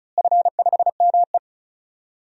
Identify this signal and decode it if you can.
Morse code, heard as a keyed tone, about 36 wpm, 700 Hz. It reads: F5ME
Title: DXCC-100 Call Signs - 4 Characters - Repeated 3x - 35wpm